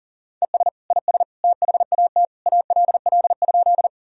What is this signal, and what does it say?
Morse code, heard as a keyed tone, about 40 wpm, 700 Hz. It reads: ES IS THAT ALL?
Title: Sentences from Top 100 Words and Abbreviations 40wpm